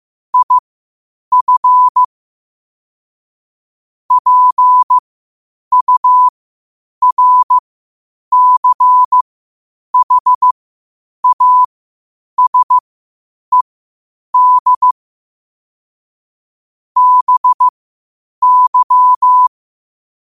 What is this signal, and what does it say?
Morse code, heard as a keyed tone, about 15 wpm, spaced as for 8 wpm, 1000 Hz. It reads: IF PURCHASED BY
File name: qotdfiles / 8wpm